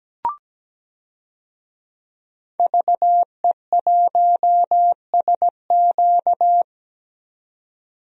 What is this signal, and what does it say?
Morse code, heard as a keyed tone, about 17 wpm, 700 Hz. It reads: VE1SQ
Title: DXCC-100 Call Signs - 5 Characters - 17wpm